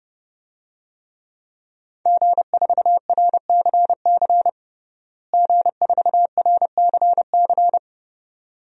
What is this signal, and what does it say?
Morse code, heard as a keyed tone, about 32 wpm, 700 Hz. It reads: G4RCC G4RCC
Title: DXCC-100 Call Signs - 5 Characters - Repeated 2x - 30wpm